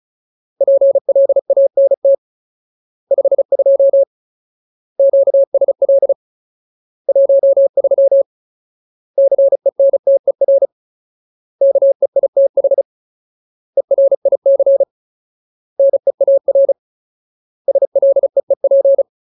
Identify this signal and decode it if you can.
Morse code, heard as a keyed tone, about 34 wpm, 550 Hz. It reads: PLANT 52 QSL 13 CENTER KEITH ERIC NEAR SLEEP
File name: Ultimate Word List / 35WPM Ultimate